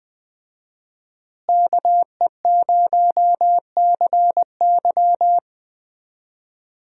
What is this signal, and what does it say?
Morse code, heard as a keyed tone, about 20 wpm, 700 Hz. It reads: KE0CY
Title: USA Call Signs - 5 Characters - Repeated 2x - 20wpm